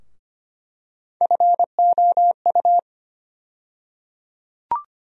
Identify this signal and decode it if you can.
Morse code, heard as a keyed tone, about 25 wpm, 700 Hz. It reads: FOU
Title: USA Call Sign Postfixes - 3 Letters 25wpm